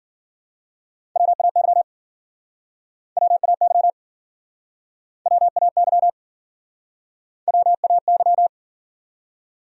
Morse code, heard as a keyed tone, about 54 wpm, 700 Hz. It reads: WAY WAY WAY WAY